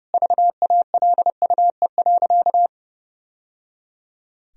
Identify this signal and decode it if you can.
Morse code, heard as a keyed tone, about 30 wpm, 700 Hz. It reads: VALUE.
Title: Sentences from Top 1000 Words - Review Part 4 - 30wpm